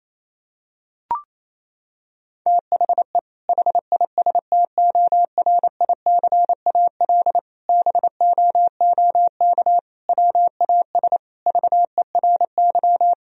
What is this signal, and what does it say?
Morse code, heard as a keyed tone, about 28 wpm, 700 Hz. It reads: THE HISTORICAL BOOK WAS VERY
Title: Sentences from Top 1000 Words - Review Part 4 - 28wpm